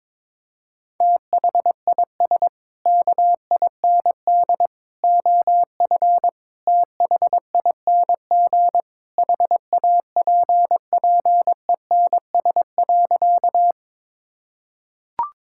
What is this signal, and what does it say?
Morse code, heard as a keyed tone, about 22 wpm, 700 Hz. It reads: THIS KIND OF THING HAPPENS.
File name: practice / 22wpm / Sentences with 3-Word Phrases - 22wpm